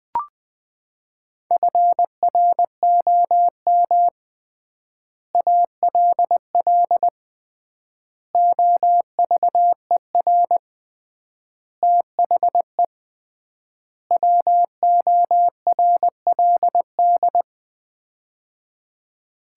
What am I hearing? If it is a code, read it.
Morse code, heard as a keyed tone, about 20 wpm, 700 Hz. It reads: FROM ALL OVER THE WORLD
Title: 3x Word-Spacing - Sets of 5 Words - 20wpm